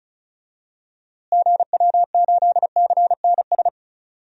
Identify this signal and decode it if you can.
Morse code, heard as a keyed tone, about 36 wpm, 700 Hz. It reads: GW8CNS
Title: DXCC-100 Call Signs - 6 Characters - Repeated 2x - 35wpm